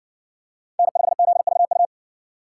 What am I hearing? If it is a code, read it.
Morse code, heard as a keyed tone, about 60 wpm, 700 Hz. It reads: N5ZVU